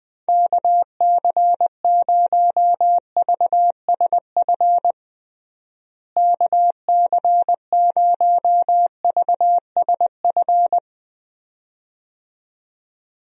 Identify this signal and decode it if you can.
Morse code, heard as a keyed tone, about 20 wpm, 700 Hz. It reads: KC0VSF KC0VSF
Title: USA Call Signs - 6 Characters - Repeated 2x - 20wpm